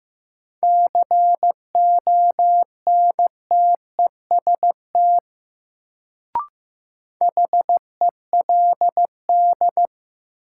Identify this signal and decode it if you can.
Morse code, heard as a keyed tone, about 15 wpm, 700 Hz. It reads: CONTEST HELD
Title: Sentences from Top 1600 Words - New Word Drill - 15wpm